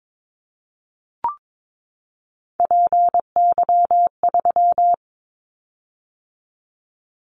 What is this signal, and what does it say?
Morse code, heard as a keyed tone, about 22 wpm, 700 Hz. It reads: PY3
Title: Callsign Prefixes for Top 100 DXCC - 22wpm